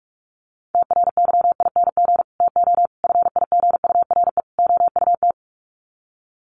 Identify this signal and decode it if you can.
Morse code, heard as a keyed tone, about 44 wpm, 700 Hz. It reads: TRYING TO FIGURE OUT
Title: Sets of 4 Words 45wpm